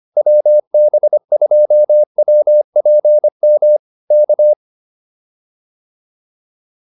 Morse code, heard as a keyed tone, about 25 wpm, 600 Hz. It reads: WB2WPM K
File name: QSO106 25